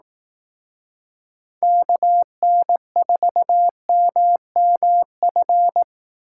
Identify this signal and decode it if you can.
Morse code, heard as a keyed tone, about 18 wpm, 700 Hz. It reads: KN4MMF
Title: Speed Racing - USA 2x3 Call Signs 15wpm